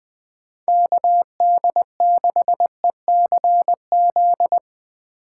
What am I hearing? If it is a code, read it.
Morse code, heard as a keyed tone, about 20 wpm, 700 Hz. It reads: KD6ECZ
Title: USA Call Signs - 6 Characters - 2x Character Spacing - 20wpm